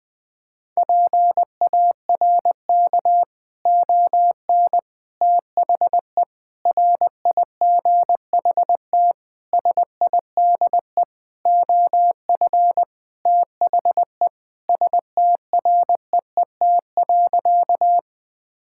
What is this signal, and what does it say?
Morse code, heard as a keyed tone, about 20 wpm, 700 Hz. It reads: PARK ON THE RIGHT SIDE OF THE STREET.
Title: Sentences with 6-Word Phrases - Spoken Beginning - 20wpm